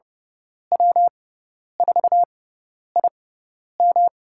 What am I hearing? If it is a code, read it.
Morse code, heard as a keyed tone, about 30 wpm, spaced as for 10 wpm, 700 Hz. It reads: W4IM
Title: USA Call Signs - 5 Characters - 6x Character Spacing - 30wpm